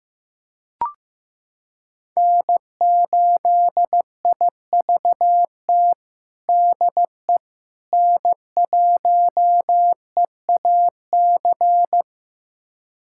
Morse code, heard as a keyed tone, about 16 wpm, 700 Hz. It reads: N8IVT DE N1EAC
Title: USA Callsign DE Callsign - 5 Characters - 15wpm